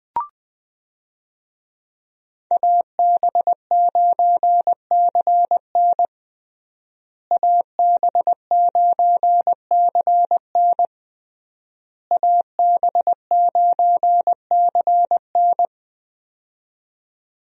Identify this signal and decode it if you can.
Morse code, heard as a keyed tone, about 20 wpm, 700 Hz. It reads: AB9CN AB9CN AB9CN
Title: USA Call Signs - 5 Characters - Repeated 3x - 20wpm